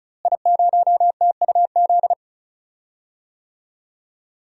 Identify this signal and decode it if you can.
Morse code, heard as a keyed tone, about 34 wpm, 700 Hz. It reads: I0TUZ
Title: DXCC-100 Call Signs - 5 Characters - Repeated 3x - 35wpm